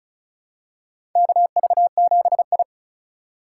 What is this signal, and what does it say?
Morse code, heard as a keyed tone, about 36 wpm, 700 Hz. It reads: KV7I
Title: USA Call Signs - 4 Characters - 2x Character Spacing - 35wpm